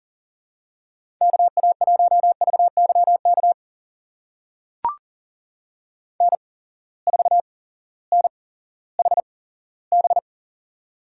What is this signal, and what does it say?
Morse code, heard as a keyed tone, about 40 wpm, spaced as for 18 wpm, 700 Hz. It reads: KA1VYK N4NHB
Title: USA Call Signs - 8x Character Spacing - 40wpm